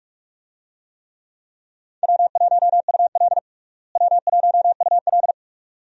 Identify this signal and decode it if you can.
Morse code, heard as a keyed tone, about 43 wpm, 700 Hz. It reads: W1UL W1UL
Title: USA Call Signs - 4 Characters - Repeated 2x - 45wpm